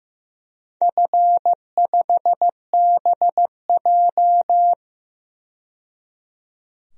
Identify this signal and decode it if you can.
Morse code, heard as a keyed tone, about 15 wpm, 700 Hz. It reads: F5BJ
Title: DXCC-100 Call Signs - 4 Characters - 15wpm